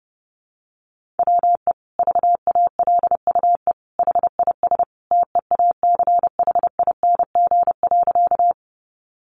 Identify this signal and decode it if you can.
Morse code, heard as a keyed tone, about 30 wpm, 700 Hz. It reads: WE VALUE HIS TEACHING.
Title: Sentences from Top 1100 Words 30wpm